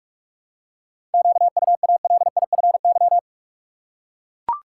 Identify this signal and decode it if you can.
Morse code, heard as a keyed tone, about 43 wpm, 700 Hz. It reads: QUALIFY
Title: Seven Letter Words - 45wpm